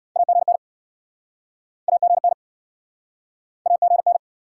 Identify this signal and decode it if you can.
Morse code, heard as a keyed tone, about 73 wpm, 700 Hz. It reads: AGN AGN AGN